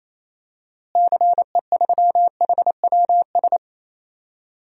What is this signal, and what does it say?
Morse code, heard as a keyed tone, about 28 wpm, 700 Hz. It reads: CE3HWS
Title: DXCC-100 Call Signs - 6 Characters - Repeated 3x - 28wpm